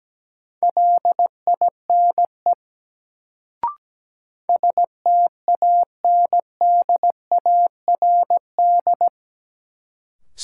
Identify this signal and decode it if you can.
Morse code, heard as a keyed tone, about 17 wpm, 700 Hz. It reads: LINE STANDARD
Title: Sentences from Top 1200 Words - New Word Drill Alt - 17wpm